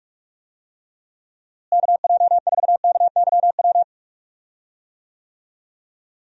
Morse code, heard as a keyed tone, about 43 wpm, 700 Hz. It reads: KJ4KYW
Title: USA Call Signs - 6 Characters - Repeated 3x - 45wpm